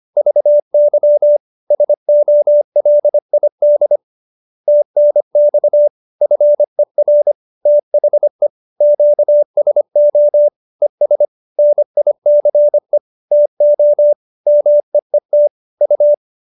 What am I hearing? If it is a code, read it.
Morse code, heard as a keyed tone, about 25 wpm, 590 Hz. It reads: VY SOLID TNX FER THE QSO ES NICE TO MEET U